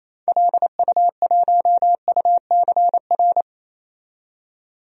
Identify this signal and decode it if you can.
Morse code, heard as a keyed tone, about 28 wpm, 700 Hz. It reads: LU1UCR
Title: DXCC-100 Call Signs - 6 Characters - 28wpm